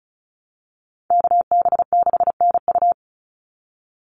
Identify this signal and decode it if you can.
Morse code, heard as a keyed tone, about 34 wpm, 700 Hz. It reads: KB6NU
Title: Call Signs Repeated 3x - 35wpm